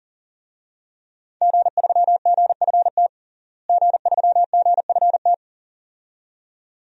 Morse code, heard as a keyed tone, about 40 wpm, 700 Hz. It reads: G3GFT G3GFT
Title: DXCC-100 Call Signs - 5 Characters - Repeated 2x - 40wpm